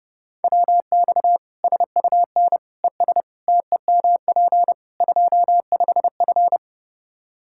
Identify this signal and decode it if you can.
Morse code, heard as a keyed tone, about 32 wpm, 700 Hz. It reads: WX SUN ES TEMP 25F
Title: Practice QSO - Exchange 2 - Info - 30wpm